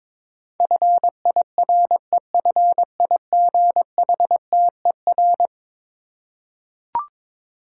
Morse code, heard as a keyed tone, about 22 wpm, 700 Hz. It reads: FIREFIGHTER
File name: practice / 22wpm / Eleven Letter Words - 22wpm